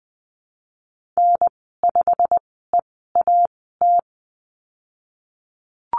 Morse code, heard as a keyed tone, about 20 wpm, spaced as for 15 wpm, 700 Hz. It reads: N5EAT
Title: 088 Lesson 15g - Number 5 - Call Signs 20wpm